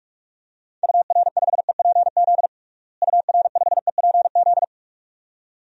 Morse code, heard as a keyed tone, about 46 wpm, 700 Hz. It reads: UR5EPZ UR5EPZ
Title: DXCC-100 Call Signs - 6 Characters - Repeated 2x - 45wpm